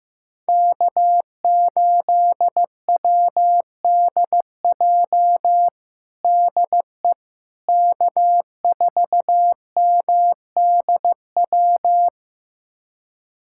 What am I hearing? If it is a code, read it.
Morse code, heard as a keyed tone, about 16 wpm, 700 Hz. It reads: K8WDJ DE K4MDW
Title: USA Callsign DE Callsign - 5 Characters - 15wpm